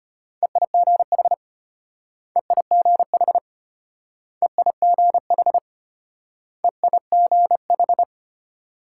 Morse code, heard as a keyed tone, about 33 wpm, 700 Hz. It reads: EIGH EIGH EIGH EIGH